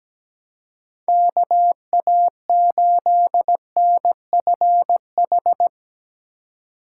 Morse code, heard as a keyed tone, about 17 wpm, 700 Hz. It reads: KA8NFH
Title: USA Call Signs - 6 Characters - 4x Character Spacing - 17wpm